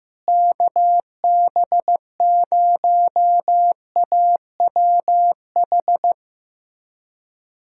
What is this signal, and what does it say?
Morse code, heard as a keyed tone, about 15 wpm, 700 Hz. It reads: KB0AWH
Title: USA Callsign DE Callsign - 6 Characters - 15wpm